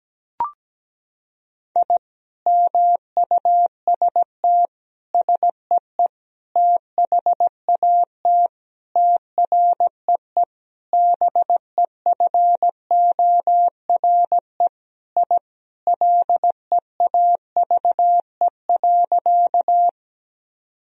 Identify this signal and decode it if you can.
Morse code, heard as a keyed tone, about 17 wpm, 700 Hz. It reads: I MUST SEE THAT TREE BEFORE I LEAVE.
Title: Sentences from Top 1000 Words - Review Part 1 - 17wpm